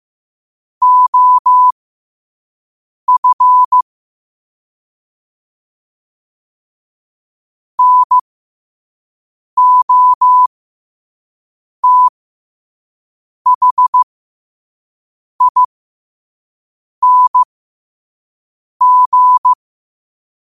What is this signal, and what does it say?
Morse code, heard as a keyed tone, about 15 wpm, spaced as for 5 wpm, 1000 Hz. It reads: OF NOTHING
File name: qotdfiles / 5wpm